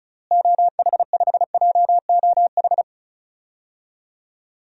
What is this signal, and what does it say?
Morse code, heard as a keyed tone, about 34 wpm, 700 Hz. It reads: OH5JOH
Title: DXCC-100 Call Signs - 6 Characters - Repeated 2x - 35wpm